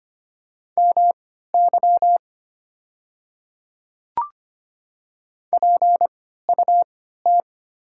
Morse code, heard as a keyed tone, about 25 wpm, spaced as for 14 wpm, 700 Hz. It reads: MY PUT
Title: 3x Character Spacing - Top 300 Words - 25wpm